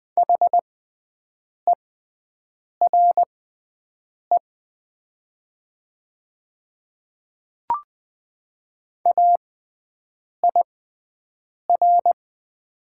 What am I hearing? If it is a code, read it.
Morse code, heard as a keyed tone, about 20 wpm, spaced as for 7 wpm, 700 Hz. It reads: HERE AIR
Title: 6x Character Spacing - Top 200 Words - 20wpm